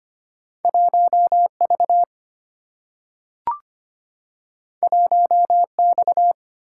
Morse code, heard as a keyed tone, about 25 wpm, 700 Hz. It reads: 1V 1X
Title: Single-Letter Single-Number 25wpm